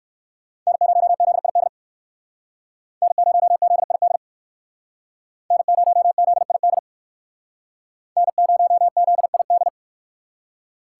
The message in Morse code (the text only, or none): N0ZID N0ZID N0ZID N0ZID